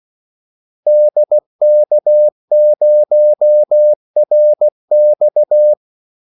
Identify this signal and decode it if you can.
Morse code, heard as a keyed tone, about 16 wpm, 600 Hz. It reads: DK0RX
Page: Kaldesignaler 16 wpm